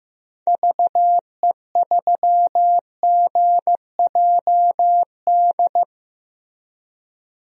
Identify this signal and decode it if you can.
Morse code, heard as a keyed tone, about 16 wpm, 700 Hz. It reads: VE3GJD
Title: DXCC-100 Call Signs - 6 Characters - Repeated 2x - 15wpm